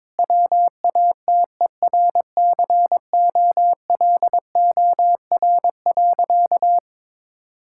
Morse code, heard as a keyed tone, about 22 wpm, 700 Hz. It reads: WATERCOLOR.
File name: Difficult Sentences with Hints - 22wpm